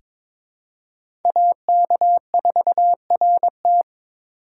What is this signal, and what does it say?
Morse code, heard as a keyed tone, about 22 wpm, 700 Hz. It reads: AK4RT
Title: USA Call Signs - 5 Characters - 2x Character Spacing - 22wpm